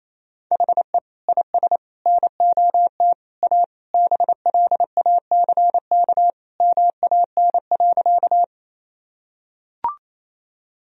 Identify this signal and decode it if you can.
Morse code, heard as a keyed tone, about 28 wpm, 700 Hz. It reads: HE IS NOT A BLACK MAN.
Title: Sentences from Top 1000 Words - Review Part 2 - 28wpm